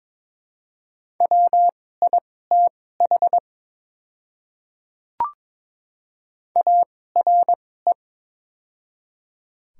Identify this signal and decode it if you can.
Morse code, heard as a keyed tone, about 22 wpm, spaced as for 16 wpm, 700 Hz. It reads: WITH ARE
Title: 2x Character Spacing - Top 100 Words - 22wpm